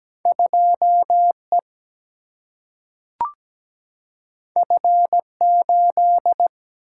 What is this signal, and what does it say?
Morse code, heard as a keyed tone, about 17 wpm, 700 Hz. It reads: 2E F8
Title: Single-Letter Single-Number 17wpm